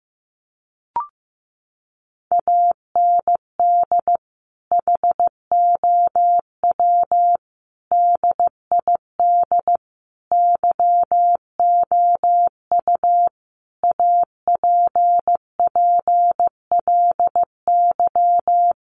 Morse code, heard as a keyed tone, about 15 wpm, 700 Hz. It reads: AND HOW DID YOU APPLY
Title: Sentences from Top 1200 Words 15wpm